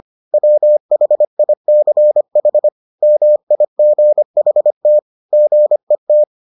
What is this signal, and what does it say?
Morse code, heard as a keyed tone, about 25 wpm, 600 Hz. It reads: WHICH MIGHT GET